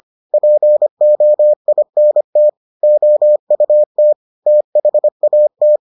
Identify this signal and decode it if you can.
Morse code, heard as a keyed tone, about 25 wpm, 600 Hz. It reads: POINT OUT THAT